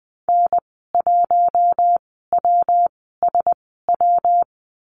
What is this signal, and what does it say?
Morse code, heard as a keyed tone, about 20 wpm, spaced as for 15 wpm, 700 Hz. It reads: N1WSW
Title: 122 Lesson 20g - Number 3 - Call Signs 20wpm